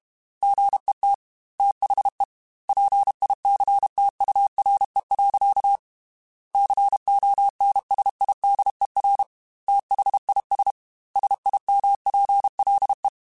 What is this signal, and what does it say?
Morse code, heard as a keyed tone, about 32 wpm, 800 Hz. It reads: GET THE PICTURE. CONSIDER THIS SIMPLE
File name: Technical Subjects / Technical Subjects-mp3s / AND Gate @30